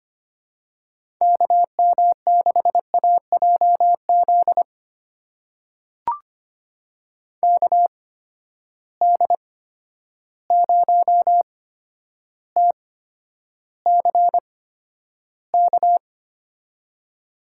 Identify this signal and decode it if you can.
Morse code, heard as a keyed tone, about 25 wpm, spaced as for 11 wpm, 700 Hz. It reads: KM6AJZ KD0TCK
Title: USA Call Signs - 6 Characters - 8x Character Spacing - 25wpm